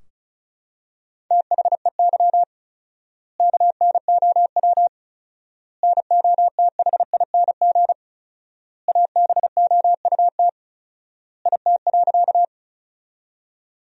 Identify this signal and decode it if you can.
Morse code, heard as a keyed tone, about 34 wpm, 700 Hz. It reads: THEY KNOW NOTHING ABOUT IT.
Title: 4x Word-Spacing - Sentences from Top 500 Words - 35wpm